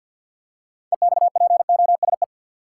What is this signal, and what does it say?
Morse code, heard as a keyed tone, about 52 wpm, 700 Hz. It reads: EXPOSE
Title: Six Letter Words - 50wpm